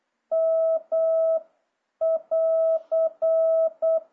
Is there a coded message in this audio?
Yes: Morse code from 0.3 to 4.0 s, heard as a keyed tone, about 8 wpm, 650 Hz. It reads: M <AR>